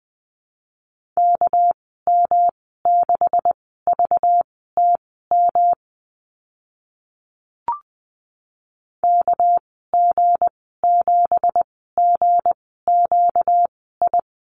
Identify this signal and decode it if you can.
Morse code, heard as a keyed tone, about 20 wpm, spaced as for 15 wpm, 700 Hz. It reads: KM6VTM KG7GQI
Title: 238 Lesson 38g - Letter q - Call Signs 20wpm